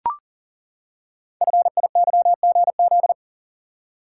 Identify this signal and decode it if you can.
Morse code, heard as a keyed tone, about 39 wpm, 690 Hz. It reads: FIYGZ
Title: Code Groups - 5 Letters - 40wpm